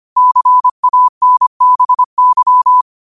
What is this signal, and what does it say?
Morse code, heard as a keyed tone, about 25 wpm, 1000 Hz. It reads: CANBY
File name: qotdfiles / 25wpm